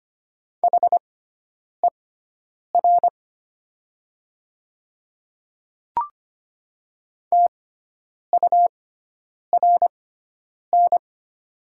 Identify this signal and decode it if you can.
Morse code, heard as a keyed tone, about 25 wpm, spaced as for 9 wpm, 700 Hz. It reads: HER TURN
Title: 6x Character Spacing - Top 200 Words - 25wpm